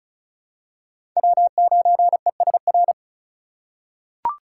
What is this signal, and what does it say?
Morse code, heard as a keyed tone, about 34 wpm, 700 Hz. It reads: W9ESR
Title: USA Call Signs - 5 Characters - 3x Character Spacing - 35wpm